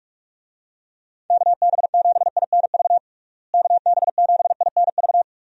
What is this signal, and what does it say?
Morse code, heard as a keyed tone, about 41 wpm, 700 Hz. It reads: KB7INV KB7INV
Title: USA Call Signs - 6 Characters - Repeated 2x - 45wpm